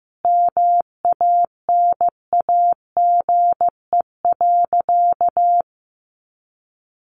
Morse code, heard as a keyed tone, about 14 wpm, 700 Hz. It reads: MANAGE.
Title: Sentences from Top 1400 Words 15wpm